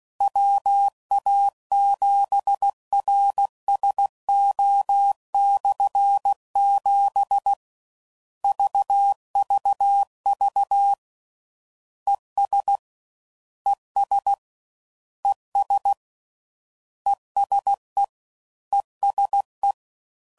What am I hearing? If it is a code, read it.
Morse code, heard as a keyed tone, about 16 wpm, 800 Hz. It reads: WA7RSO/7 VVV ES ES ES ESE ESE